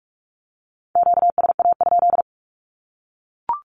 Morse code, heard as a keyed tone, about 44 wpm, 700 Hz. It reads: QSA?
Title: Q-Codes Repeated 3x 45wpm